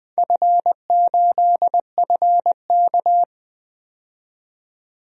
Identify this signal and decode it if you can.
Morse code, heard as a keyed tone, about 20 wpm, 700 Hz. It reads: F8FK
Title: DXCC-100 Call Signs - 4 Characters - 20wpm